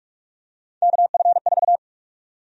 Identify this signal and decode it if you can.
Morse code, heard as a keyed tone, about 44 wpm, 700 Hz. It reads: KF4
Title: USA Call Sign Prefixes - 45wpm